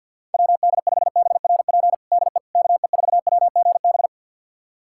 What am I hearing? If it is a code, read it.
Morse code, heard as a keyed tone, about 46 wpm, 700 Hz. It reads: WD5BRP DE KE4WGB
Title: USA Callsign DE Callsign 50wpm